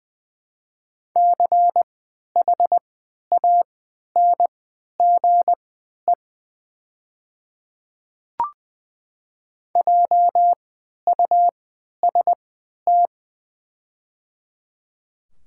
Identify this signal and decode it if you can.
Morse code, heard as a keyed tone, about 21 wpm, spaced as for 11 wpm, 700 Hz. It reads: CHANGE JUST
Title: 3x Character Spacing - Top 200 Words - 20wpm